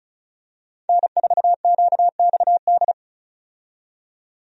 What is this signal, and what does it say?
Morse code, heard as a keyed tone, about 36 wpm, 700 Hz. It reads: N4QXD
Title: USA Call Signs - 5 Characters - 35wpm